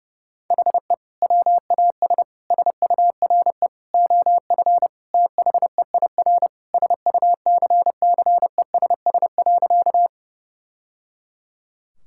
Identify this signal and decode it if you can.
Morse code, heard as a keyed tone, about 29 wpm, 700 Hz. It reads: HE WAS SURE OF THEIR SUCCESS.
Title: Sentences from Top 1000 Words - Review Part 6 - 30wpm